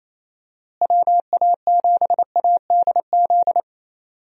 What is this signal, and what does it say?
Morse code, heard as a keyed tone, about 28 wpm, 700 Hz. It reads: WA7ADZ
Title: USA Call Signs - 6 Characters - Repeated 2x - 28wpm